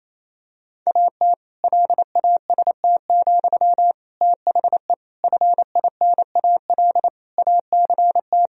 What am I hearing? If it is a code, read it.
Morse code, heard as a keyed tone, about 28 wpm, 700 Hz. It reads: AT LAST, THE FINAL ACT
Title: Sentences from Top 1000 Words - Review Part 2 - 28wpm